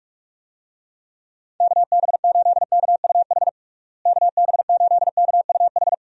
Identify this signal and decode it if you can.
Morse code, heard as a keyed tone, about 46 wpm, 700 Hz. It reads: KB8KUH KB8KUH
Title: USA Call Signs - 6 Characters - Repeated 2x - 45wpm